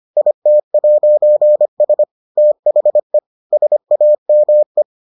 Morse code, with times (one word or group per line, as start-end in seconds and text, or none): 0.2-2.0 IT'S
2.4-3.2 THE
3.5-4.8 SAME